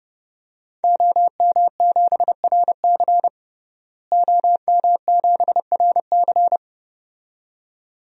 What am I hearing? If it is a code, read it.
Morse code, heard as a keyed tone, about 30 wpm, 700 Hz. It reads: OM7RC OM7RC
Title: DXCC-100 Call Signs - 5 Characters - Repeated 2x - 30wpm